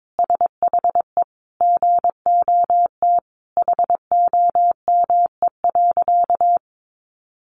Morse code, heard as a keyed tone, about 22 wpm, 700 Hz. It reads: SHE GOT HOME.
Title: Sentences from Top 1400 Words 22wpm